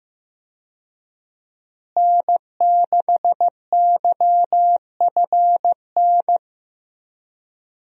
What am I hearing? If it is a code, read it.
Morse code, heard as a keyed tone, about 15 wpm, 700 Hz. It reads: N6YFN
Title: USA Call Signs - 5 Characters - Repeated 3x - 15wpm